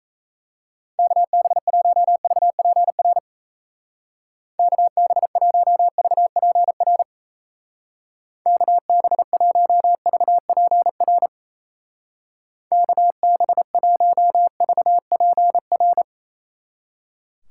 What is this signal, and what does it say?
Morse code, heard as a keyed tone, about 35 wpm, 700 Hz. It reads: KB1VPR KB1VPR KB1VPR KB1VPR